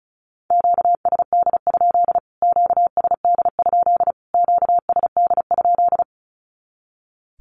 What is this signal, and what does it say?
Morse code, heard as a keyed tone, about 34 wpm, 700 Hz. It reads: QSD? QSD? QSD?
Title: Q-Codes Repeated 3x 35wpm